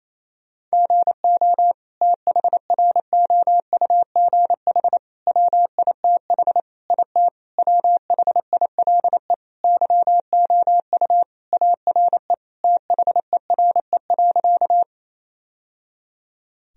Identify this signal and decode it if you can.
Morse code, heard as a keyed tone, about 28 wpm, 700 Hz. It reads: GO THROUGH WITH IT WHILE YOU ARE THERE.
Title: Sentences from Top 1000 Words - Review Part 1 - 28wpm